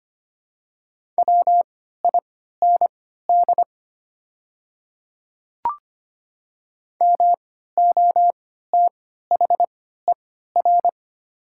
Morse code, heard as a keyed tone, about 25 wpm, spaced as for 14 wpm, 700 Hz. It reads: WIND MOTHER
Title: 3x Character Spacing - Top 500 Words - 25wpm